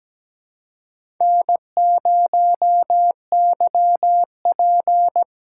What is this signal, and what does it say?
Morse code, heard as a keyed tone, about 17 wpm, 700 Hz. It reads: N0YP
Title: USA Call Signs - 4 Characters - 6x Character Spacing - 17wpm